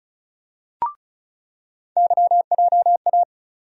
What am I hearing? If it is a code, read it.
Morse code, heard as a keyed tone, about 36 wpm, 700 Hz. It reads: YJA
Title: USA Call Sign Postfixes - 3 Letters 35wpm